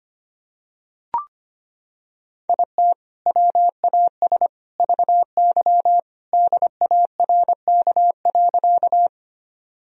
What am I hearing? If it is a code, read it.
Morse code, heard as a keyed tone, about 25 wpm, 700 Hz. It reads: IT WAS VY DARK.
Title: Sentences from Top 500 Words and Abbreviations 25wpm